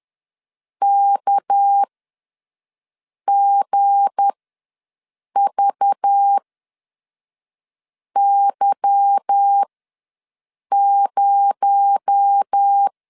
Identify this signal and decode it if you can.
Morse code, heard as a keyed tone, about 10 wpm, spaced as for 5 wpm, 800 Hz. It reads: KGVY0
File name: LEZIONI / 9 2 ws 10wpm 4130